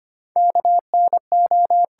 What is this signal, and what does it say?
Morse code, heard as a keyed tone, about 25 wpm, 700 Hz. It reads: KNO
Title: Sets of 2 Words - New Word Drill - 25wpm